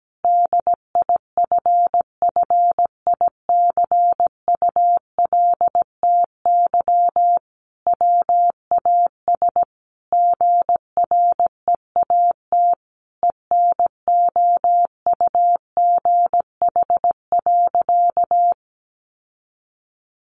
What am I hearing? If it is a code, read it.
Morse code, heard as a keyed tone, about 17 wpm, 700 Hz. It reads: DIFFICULTY WAS GREAT ENOUGH.
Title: Sentences from Top 1500 Words 17wpm